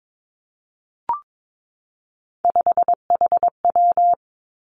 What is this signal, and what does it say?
Morse code, heard as a keyed tone, about 22 wpm, 700 Hz. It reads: 5HW
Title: Three Challenging Letter-Number Combinations 22wpm